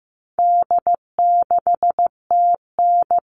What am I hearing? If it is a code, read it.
Morse code, heard as a keyed tone, about 15 wpm, 700 Hz. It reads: D6TN
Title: All Cut Numbers - 4-Digit 15wpm